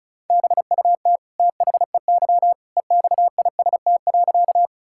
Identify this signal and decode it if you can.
Morse code, heard as a keyed tone, about 36 wpm, 700 Hz. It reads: BUT THEY EXIST.
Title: Sentences from Top 1000 Words - Review Part 5 - 35wpm